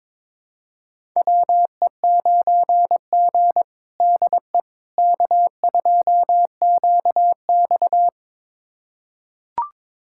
Morse code, heard as a keyed tone, about 22 wpm, 700 Hz. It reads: WE9G DE K2QX
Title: USA Callsign DE Callsign - 4 Characters - 22wpm